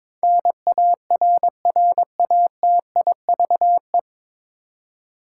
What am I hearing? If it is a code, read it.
Morse code, heard as a keyed tone, about 22 wpm, 700 Hz. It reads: NARRATIVE
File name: 9 Letter Words - First 3 Letters Given - 22wpm